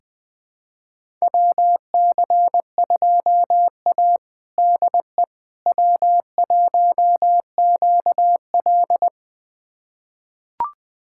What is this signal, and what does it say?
Morse code, heard as a keyed tone, about 20 wpm, 700 Hz. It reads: WC2A DE W1QL
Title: USA Callsign DE Callsign - 4 Characters - 20wpm